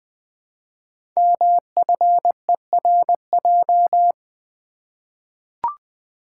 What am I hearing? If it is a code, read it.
Morse code, heard as a keyed tone, about 20 wpm, 700 Hz. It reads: MFERJ
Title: Code Groups - 5 Letters - 20wpm